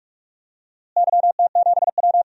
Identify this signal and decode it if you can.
Morse code, heard as a keyed tone, about 46 wpm, 700 Hz. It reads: YT7W
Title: DXCC-100 Call Signs - 4 Characters - Repeated 2x - 45wpm